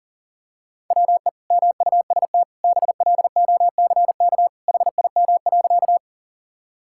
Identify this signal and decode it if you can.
Morse code, heard as a keyed tone, about 40 wpm, 700 Hz. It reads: WE MUST BLOCK HIM.
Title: Sentences from Top 1600 Words 40wpm